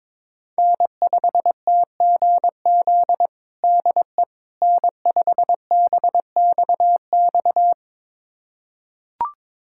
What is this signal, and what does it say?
Morse code, heard as a keyed tone, about 22 wpm, 700 Hz. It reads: N5TGZ DE N5BXX
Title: USA Callsign DE Callsign - 5 Characters - 22wpm